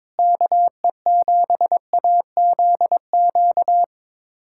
Morse code, heard as a keyed tone, about 22 wpm, 700 Hz. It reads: KE7AZQ
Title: USA Callsign DE Callsign - 6 Characters - 22wpm